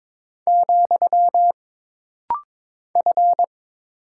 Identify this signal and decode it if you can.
Morse code, heard as a keyed tone, about 22 wpm, 700 Hz. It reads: , F
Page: This rapid-fire timing is geared to push your ICR (Instant Character Recognition) skills.